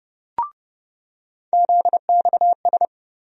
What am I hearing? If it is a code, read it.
Morse code, heard as a keyed tone, about 30 wpm, 700 Hz. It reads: ZXS
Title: USA Call Sign Postfixes - 3 Letters 30wpm